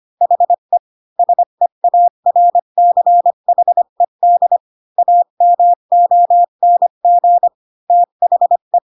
Morse code, heard as a keyed tone, about 25 wpm, 700 Hz. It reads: HE SEARCHED AMONG THE
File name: Pinocchio / Pinocchio ch1 20 25 ik1hge